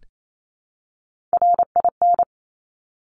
Morse code, heard as a keyed tone, about 28 wpm, 700 Hz. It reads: RIN
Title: Top Three Letter Combinations in Words 28wpm